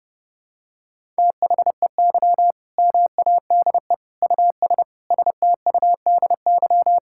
Morse code, heard as a keyed tone, about 30 wpm, 700 Hz. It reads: THEY MADE US STUDY
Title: Sentences from Top 1000 Words - Review Part 6 - 30wpm